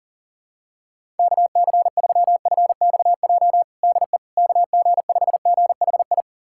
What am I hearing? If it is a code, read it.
Morse code, heard as a keyed tone, about 40 wpm, 700 Hz. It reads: KC3FXJ DE KG5GHI
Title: USA Callsign DE Callsign 40wpm